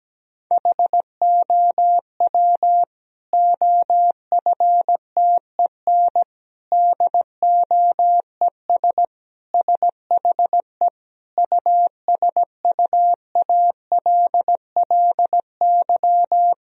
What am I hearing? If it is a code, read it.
Morse code, heard as a keyed tone, about 17 wpm, 700 Hz. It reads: HOW OFTEN DOES SHE USUALLY
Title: Sentences from Top 1000 Words - Review Part 3 - 17wpm